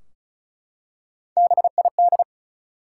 Morse code, heard as a keyed tone, about 36 wpm, 700 Hz. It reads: BID
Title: Three Letter Words - 35wpm